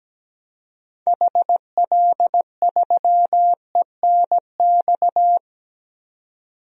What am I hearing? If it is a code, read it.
Morse code, heard as a keyed tone, about 17 wpm, 700 Hz. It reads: HL3ENX
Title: DXCC-100 Call Signs - 6 Characters - 17wpm